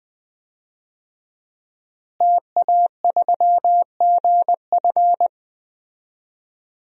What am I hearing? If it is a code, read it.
Morse code, heard as a keyed tone, about 20 wpm, 700 Hz. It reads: TA3GF
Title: DXCC-100 Call Signs - 5 Characters - Repeated 3x - 20wpm